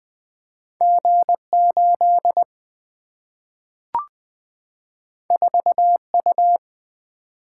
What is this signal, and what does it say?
Morse code, heard as a keyed tone, about 20 wpm, 700 Hz. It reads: G8 4U
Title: Single-Letter Single-Number 20wpm